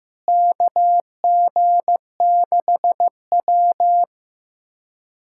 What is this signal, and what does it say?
Morse code, heard as a keyed tone, about 15 wpm, 700 Hz. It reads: KG6W
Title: USA Callsign DE Callsign - 4 Characters - 15wpm